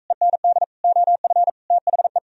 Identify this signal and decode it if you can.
Morse code, heard as a keyed tone, about 46 wpm, 700 Hz. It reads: END OF THE